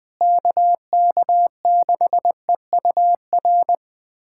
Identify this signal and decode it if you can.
Morse code, heard as a keyed tone, about 21 wpm, 700 Hz. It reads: KK6EUR
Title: USA Callsign DE Callsign - 6 Characters - 20wpm